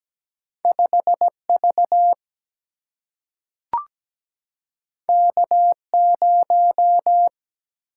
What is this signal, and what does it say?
Morse code, heard as a keyed tone, about 17 wpm, 700 Hz. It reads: 5V K0
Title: Single-Letter Single-Number 17wpm